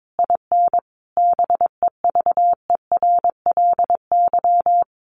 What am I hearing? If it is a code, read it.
Morse code, heard as a keyed tone, about 22 wpm, 700 Hz. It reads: IN BEVERLY
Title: Easy Phrases 22wpm